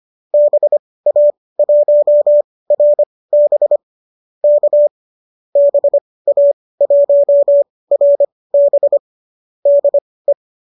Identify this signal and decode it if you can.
Morse code, heard as a keyed tone, about 25 wpm, spaced as for 18 wpm, 590 Hz. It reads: BA1RB K BA1RB DE